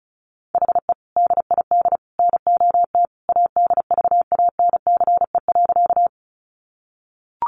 Morse code, heard as a keyed tone, about 34 wpm, 700 Hz. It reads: HE DID NOT ADVANCE.
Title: Sentences from Top 1400 Words 35wpm